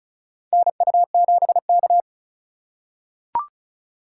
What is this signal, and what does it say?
Morse code, heard as a keyed tone, about 36 wpm, 700 Hz. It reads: NU7K
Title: USA Call Signs - 4 Characters - 4x Character Spacing - 35wpm